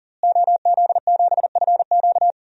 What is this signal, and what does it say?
Morse code, heard as a keyed tone, about 40 wpm, 700 Hz. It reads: OZ7FQ
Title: DXCC-100 Call Signs - 5 Characters - Repeated 2x - 40wpm